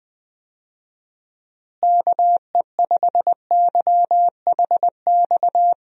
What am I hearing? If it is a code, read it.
Morse code, heard as a keyed tone, about 20 wpm, 700 Hz. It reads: KE5YHX
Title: USA Call Signs - 6 Characters - Repeated 3x - 20wpm